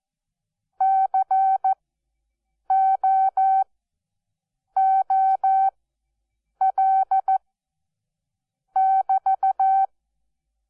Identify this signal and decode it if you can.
Morse code, heard as a keyed tone, about 14 wpm, spaced as for 6 wpm, 770 Hz. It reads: COOL<BT>
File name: Onlinelehrgang / morsen / DJ4UF Morsen12